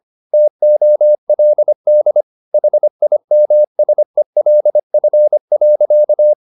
Morse code, heard as a keyed tone, about 25 wpm, 600 Hz. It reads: TOLD HIMSELF.